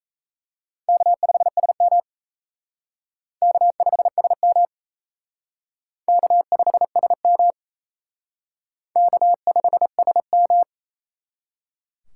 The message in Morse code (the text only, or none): K5SM K5SM K5SM K5SM